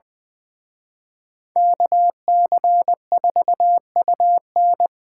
Morse code, heard as a keyed tone, about 23 wpm, 700 Hz. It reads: KC4UN
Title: Speed Racing - USA 2x2 Call Signs 20wpm